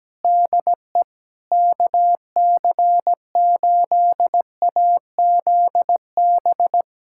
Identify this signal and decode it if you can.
Morse code, heard as a keyed tone, about 17 wpm, 700 Hz. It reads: DE KC8AZB
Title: Practice QSO - Exchange 4 - Ending - 17wpm